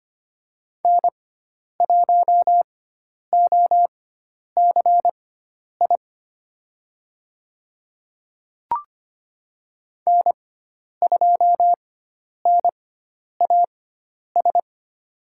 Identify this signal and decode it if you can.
Morse code, heard as a keyed tone, about 25 wpm, spaced as for 10 wpm, 700 Hz. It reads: N1OCI N2NAS
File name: Lesson 16g - Letter c - Call Signs 25wpm